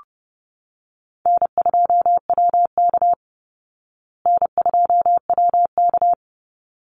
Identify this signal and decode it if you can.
Morse code, heard as a keyed tone, about 30 wpm, 700 Hz. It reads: N2WK N2WK
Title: Call Signs Repeated 3x - 30wpm